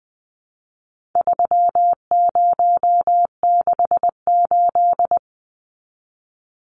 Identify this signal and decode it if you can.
Morse code, heard as a keyed tone, about 20 wpm, 700 Hz. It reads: 3068
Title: Numbers - 4-Digit 20wpm